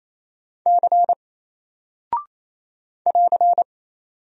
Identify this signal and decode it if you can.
Morse code, heard as a keyed tone, about 28 wpm, 700 Hz. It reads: C <AR>
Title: Single Character (Letter:Number:Punctuation:Prosign) Alt 28wpm